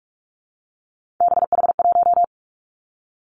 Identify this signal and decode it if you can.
Morse code, heard as a keyed tone, about 46 wpm, 700 Hz. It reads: BH1
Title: Three Challenging Letter-Number Combinations 45wpm